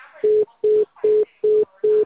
Congestion tone (425 Hz).